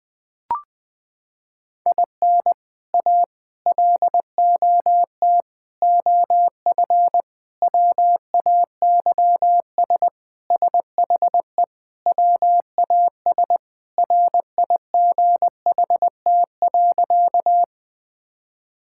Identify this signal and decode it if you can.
Morse code, heard as a keyed tone, about 21 wpm, 700 Hz. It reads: IN A LOT OF WAYS SHE WAS RIGHT.
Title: Sentences with 4-Word Phrases - 20wpm